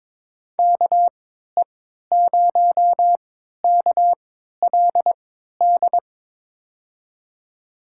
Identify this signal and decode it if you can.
Morse code, heard as a keyed tone, about 22 wpm, spaced as for 12 wpm, 700 Hz. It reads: KE0KLD
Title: USA Call Signs - 6 Characters - 3x Character Spacing - 22wpm